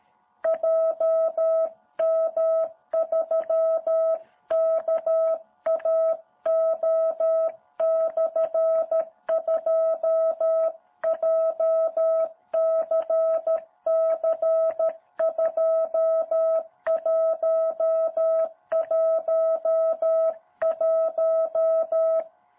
Morse code, heard as a keyed tone, about 13 wpm, 660 Hz. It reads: JM3KAO/2 JCC2111
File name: jh8cbh / 2021cw